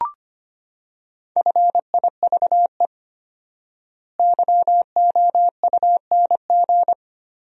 Morse code, heard as a keyed tone, about 25 wpm, 700 Hz. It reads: FIVE YOUNG
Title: 4x Word-Spacing - Sentences from Top 400 Words - 25wpm